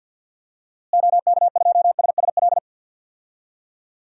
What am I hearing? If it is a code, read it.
Morse code, heard as a keyed tone, about 52 wpm, 700 Hz. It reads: OK2SSL
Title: DXCC-100 Call Signs - 6 Characters - Repeated 2x - 50wpm